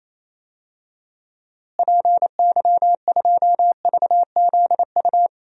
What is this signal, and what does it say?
Morse code, heard as a keyed tone, about 27 wpm, 700 Hz. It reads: PY2VZU
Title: DXCC-100 Call Signs - 6 Characters - Repeated 3x - 28wpm